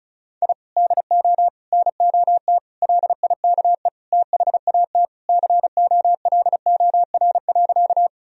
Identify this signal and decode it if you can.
Morse code, heard as a keyed tone, about 36 wpm, 700 Hz. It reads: I DO NOT LIKE THAT COLOR.
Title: Sentences from Top 1000 Words - Review Part 2 - 35wpm